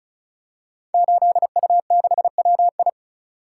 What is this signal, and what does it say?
Morse code, heard as a keyed tone, about 36 wpm, 690 Hz. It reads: 8U6WI
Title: Code Groups - 5 Letter:Numbers - 35wpm